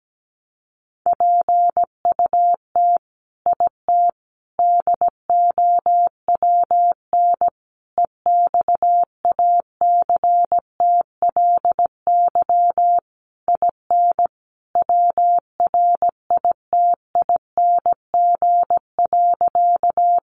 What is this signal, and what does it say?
Morse code, heard as a keyed tone, about 17 wpm, 700 Hz. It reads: PUT IT DOWN EXACTLY IN WRITING.
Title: Sentences from Top 900 Words 17wpm